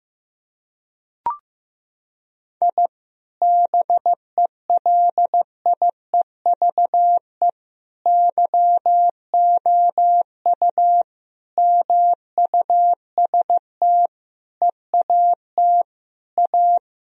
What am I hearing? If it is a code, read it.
Morse code, heard as a keyed tone, about 16 wpm, 700 Hz. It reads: I BELIEVE YOU MUST EAT A
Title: Sentences from Top 1000 Words - Review Part 3 - 15wpm